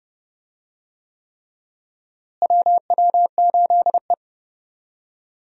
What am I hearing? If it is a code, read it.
Morse code, heard as a keyed tone, about 30 wpm, 700 Hz. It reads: WW8E
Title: USA Call Signs - 4 Characters - Repeated 3x - 30wpm